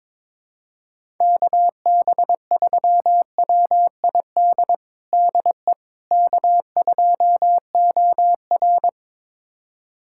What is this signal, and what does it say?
Morse code, heard as a keyed tone, about 22 wpm, 700 Hz. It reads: KB3WID DE K2OR
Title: USA Callsign DE Callsign 22wpm